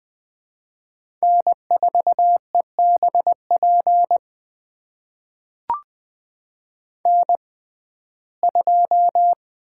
Morse code, heard as a keyed tone, about 20 wpm, spaced as for 14 wpm, 700 Hz. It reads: N4EBP N2
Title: USA Call Signs - 5 Characters - 6x Character Spacing - 20wpm